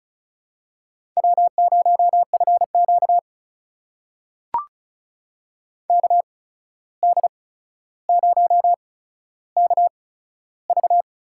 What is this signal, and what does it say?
Morse code, heard as a keyed tone, about 36 wpm, spaced as for 14 wpm, 700 Hz. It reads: W0FQ KD0KV
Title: USA Call Signs - 8x Character Spacing - 35wpm